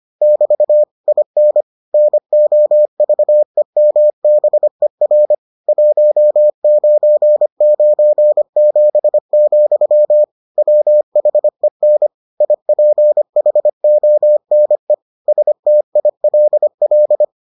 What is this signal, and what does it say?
Morse code, heard as a keyed tone, about 25 wpm, 600 Hz. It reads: <BT> IN NOVEMBER 1997, WHEN IPHONE STILL